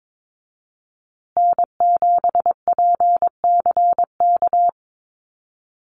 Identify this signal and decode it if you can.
Morse code, heard as a keyed tone, about 22 wpm, 700 Hz. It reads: N7PCK
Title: USA Call Signs Repeated 2x 22wpm